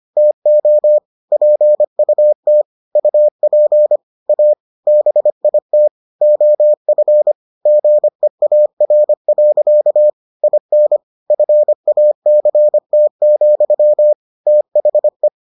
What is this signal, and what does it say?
Morse code, heard as a keyed tone, about 25 wpm, 600 Hz. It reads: TO PUT UP A BIT OF GEAR. IN FACT, THE